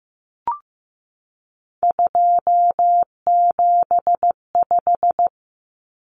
Morse code, heard as a keyed tone, about 15 wpm, 700 Hz. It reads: 275
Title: Common Cut Numbers - 3-Digit 15wpm